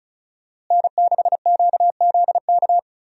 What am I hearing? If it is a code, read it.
Morse code, heard as a keyed tone, about 36 wpm, 700 Hz. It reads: N6QZK
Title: USA Call Signs - 5 Characters - Repeated 3x - 35wpm